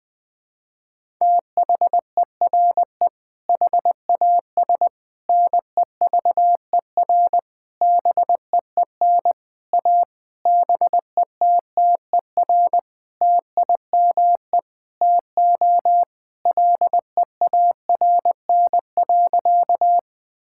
Morse code, heard as a keyed tone, about 20 wpm, 700 Hz. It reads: THERE HAS NEVER BEEN A BETTER TIME TO LEARN.
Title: Sentences with 8-Word Phrases - 20wpm